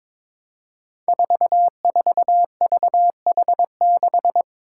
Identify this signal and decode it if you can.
Morse code, heard as a keyed tone, about 22 wpm, 700 Hz. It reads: 44VH6
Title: Five Challenging Letter-Number Combinations 22wpm